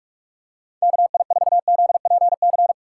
Morse code, heard as a keyed tone, about 45 wpm, 700 Hz. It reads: KI4ZPC
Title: USA Call Signs - 6x Character Spacing - 45wpm